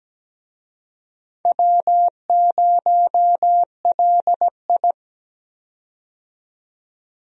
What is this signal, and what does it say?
Morse code, heard as a keyed tone, about 17 wpm, 700 Hz. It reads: W0LI
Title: USA Call Signs - 4 Characters - 17wpm